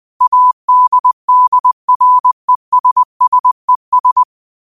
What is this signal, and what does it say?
Morse code, heard as a keyed tone, about 20 wpm, 1000 Hz. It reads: ADDRESSES
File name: qotdfiles / 20wpm